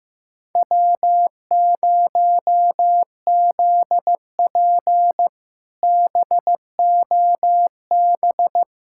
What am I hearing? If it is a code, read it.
Morse code, heard as a keyed tone, about 15 wpm, 700 Hz. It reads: W0ZP BOB
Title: K1USN SST QSO Practice - 15wpm